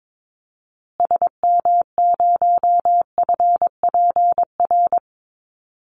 Morse code, heard as a keyed tone, about 22 wpm, 700 Hz. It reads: SM0FPR
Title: Call Signs Repeated 2x - 22wpm